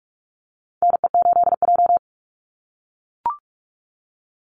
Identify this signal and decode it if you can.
Morse code, heard as a keyed tone, about 44 wpm, 700 Hz. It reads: NE8J
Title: Call Signs - 45wpm